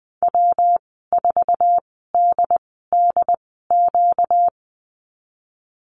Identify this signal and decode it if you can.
Morse code, heard as a keyed tone, about 20 wpm, 700 Hz. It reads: W 4 D D Q
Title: 238 Lesson 38g - Letter q - Call Signs 20wpm